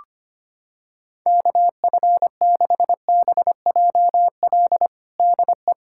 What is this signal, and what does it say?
Morse code, heard as a keyed tone, about 25 wpm, 700 Hz. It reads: KF6BJL DE
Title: USA Callsign DE Callsign - 6 Characters - 25wpm